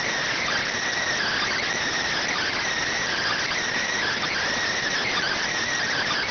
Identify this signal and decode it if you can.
Emergency Alert System (SAME) tones (header bursts), heard as harsh buzzing bursts. It is heard from 1.2 to 5.4 s.
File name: findings / CT CORONAL 710mm 10mm scout.gen-35 rx